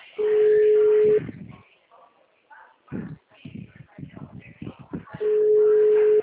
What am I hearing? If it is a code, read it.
Ringback tone (Europe, 425 Hz).